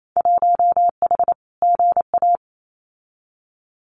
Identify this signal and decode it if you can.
Morse code, heard as a keyed tone, about 28 wpm, 700 Hz. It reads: 1H GA
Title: Winter Field Day Exchanges 28wpm